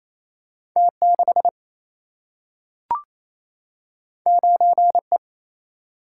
Morse code, heard as a keyed tone, about 28 wpm, 700 Hz. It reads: T6 9E
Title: Single-Letter Single-Number 28wpm